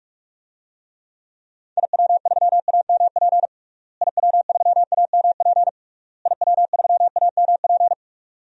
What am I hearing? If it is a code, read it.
Morse code, heard as a keyed tone, about 41 wpm, 700 Hz. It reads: IW3AMP IW3AMP IW3AMP
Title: DXCC-100 Call Signs - 6 Characters - Repeated 3x - 45wpm